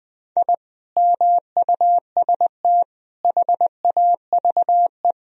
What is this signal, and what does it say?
Morse code, heard as a keyed tone, about 21 wpm, 700 Hz. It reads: I MUST HAVE
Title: Sentences from Top 1000 Words - Review Part 4 - 20wpm